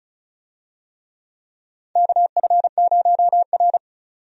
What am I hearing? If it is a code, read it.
Morse code, heard as a keyed tone, about 34 wpm, 700 Hz. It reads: KF0R
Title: USA Call Signs - 4 Characters - 35wpm